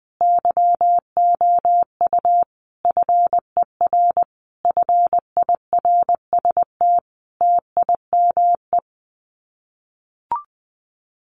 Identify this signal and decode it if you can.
Morse code, heard as a keyed tone, about 19 wpm, 700 Hz. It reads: YOU FER FIRST TIME
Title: Common QSO Phrases 20wpm